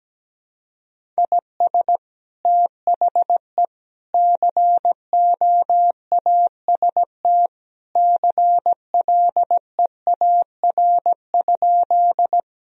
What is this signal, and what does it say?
Morse code, heard as a keyed tone, about 17 wpm, 700 Hz. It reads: IS THE COAST CLEAR?
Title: Sentences from Top 1000 Words - Review Part 4 - 17wpm